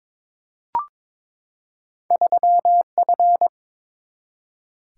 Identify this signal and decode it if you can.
Morse code, heard as a keyed tone, about 22 wpm, 690 Hz. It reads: 3F
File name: practice / 22wpm / Two Letter-Number Combinations - 22wpm